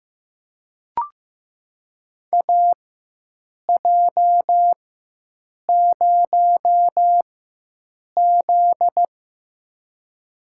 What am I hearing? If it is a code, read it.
Morse code, heard as a keyed tone, about 15 wpm, spaced as for 7 wpm, 700 Hz. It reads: AJ0Z
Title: USA Call Signs - 4 Characters - 4x Character Spacing - 15wpm